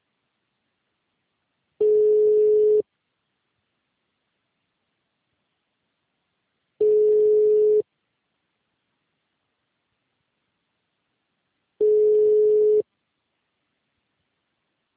Ringback tone (Europe, 425 Hz).